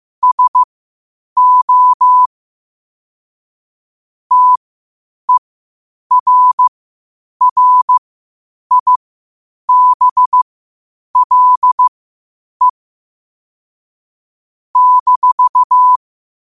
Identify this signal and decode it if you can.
Morse code, heard as a keyed tone, about 15 wpm, spaced as for 8 wpm, 1000 Hz. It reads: SO TERRIBLE -
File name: qotdfiles / 8wpm